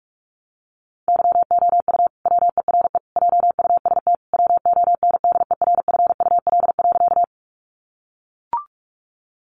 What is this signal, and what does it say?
Morse code, heard as a keyed tone, about 44 wpm, 700 Hz. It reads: YOU WERE JUST WONDERFUL.
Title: Sentences from Top 1100 Words 45wpm